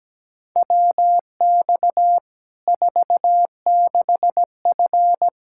Morse code, heard as a keyed tone, about 17 wpm, 700 Hz. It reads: WX 46F
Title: Practice QSO - Exchange 2 - Info - 17wpm